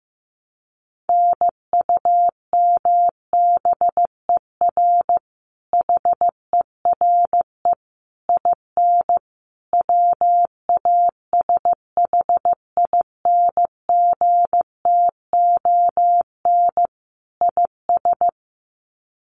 Morse code, heard as a keyed tone, about 15 wpm, 700 Hz. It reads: NUMBER HERE IN WASHINGTON IS
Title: Sets of 5 Words Encore2 15wpm